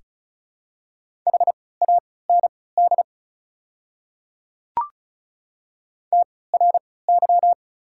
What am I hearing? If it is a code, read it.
Morse code, heard as a keyed tone, about 36 wpm, spaced as for 20 wpm, 700 Hz. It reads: HAND TRY
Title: 3x Character Spacing - Top 300 Words - 35wpm